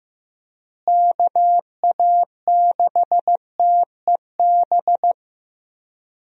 Morse code, heard as a keyed tone, about 16 wpm, 700 Hz. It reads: KA6TEB
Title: USA Call Signs - 6 Characters - Repeated 3x - 15wpm